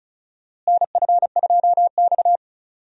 Morse code, heard as a keyed tone, about 34 wpm, 700 Hz. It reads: NF2X
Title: USA Call Signs - 4 Characters - Repeated 2x - 35wpm